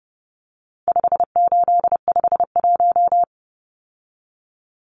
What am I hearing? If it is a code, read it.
Morse code, heard as a keyed tone, about 30 wpm, 700 Hz. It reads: 5851
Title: Numbers - 4-Digit 30wpm